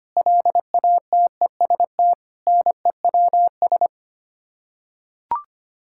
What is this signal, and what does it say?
Morse code, heard as a keyed tone, about 25 wpm, 700 Hz. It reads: LATEST NEWS
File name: Sets of 9 Words - 25wpm